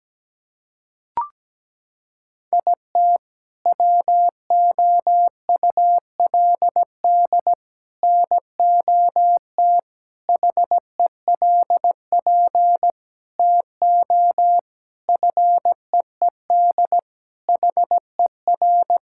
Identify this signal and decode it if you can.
Morse code, heard as a keyed tone, about 17 wpm, 700 Hz. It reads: IT WOULD NOT HELP TO FEED HER
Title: Sentences from Top 1600 Words 17wpm